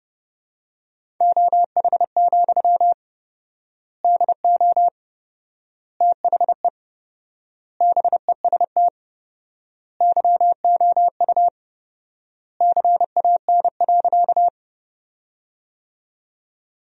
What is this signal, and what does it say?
Morse code, heard as a keyed tone, about 32 wpm, 700 Hz. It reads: OH, DO THE BEST YOU CAN.
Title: 4x Word-Spacing - Sentences from Top 500 Words - 30wpm